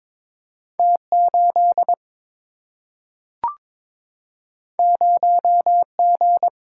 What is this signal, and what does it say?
Morse code, heard as a keyed tone, about 22 wpm, 700 Hz. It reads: T 8 0 G
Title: Single-Letter Single-Number 22wpm